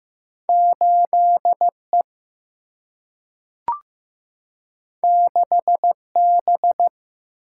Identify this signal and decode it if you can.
Morse code, heard as a keyed tone, about 15 wpm, 700 Hz. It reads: 8 E 6 B
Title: Single-Letter Single-Number 15wpm